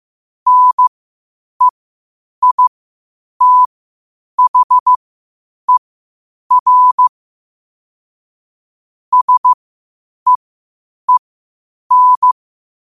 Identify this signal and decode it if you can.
Morse code, heard as a keyed tone, about 15 wpm, spaced as for 8 wpm, 1000 Hz. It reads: NEITHER SEEN
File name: qotdfiles / 8wpm